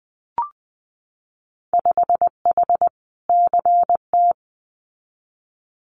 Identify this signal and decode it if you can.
Morse code, heard as a keyed tone, about 20 wpm, 700 Hz. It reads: 5H CT
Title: Winter Field Day Exchanges Encore 20wpm